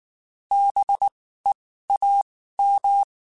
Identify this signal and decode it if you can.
Morse code, heard as a keyed tone, about 19 wpm, spaced as for 14 wpm, 800 Hz. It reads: BEAM
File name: (spaced) / QSOs Spaced-mp3s / WD0FBT @18 sp